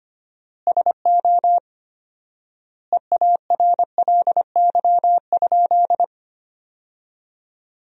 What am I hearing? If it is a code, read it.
Morse code, heard as a keyed tone, about 25 wpm, 700 Hz. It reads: SO EARLY?
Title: 4x Word-Spacing - Sentences from Top 400 Words - 25wpm